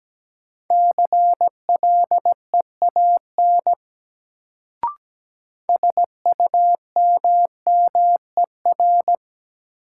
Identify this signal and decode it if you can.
Morse code, heard as a keyed tone, about 17 wpm, 700 Hz. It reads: CLEAN SUMMER
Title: Sentences from Top 1100 Words - New Word Drill Alt - 17wpm